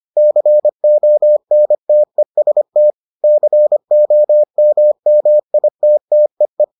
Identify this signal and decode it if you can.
Morse code, heard as a keyed tone, about 25 wpm, 600 Hz. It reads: CONTEST COMMITTEE